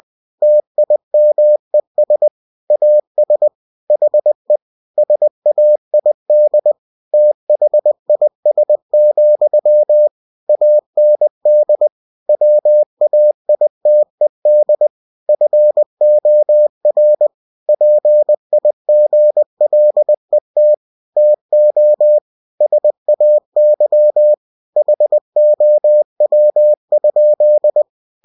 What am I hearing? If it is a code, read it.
Morse code, heard as a keyed tone, about 20 wpm, 600 Hz. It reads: TIMES AS HE SAID THIS, AND WAITED FOR PIGLET TO SAY HOW?
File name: cw-books / winnie-the-pooh / winnie-20wpm-page-0034